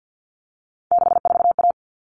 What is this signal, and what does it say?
Morse code, heard as a keyed tone, about 50 wpm, 700 Hz. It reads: 64A
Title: All Cut Numbers - 3-Digit 50wpm